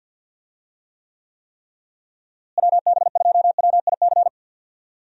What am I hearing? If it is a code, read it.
Morse code, heard as a keyed tone, about 53 wpm, 700 Hz. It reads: WB2WIC
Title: USA Call Signs - 6 Characters - Repeated 3x - 50wpm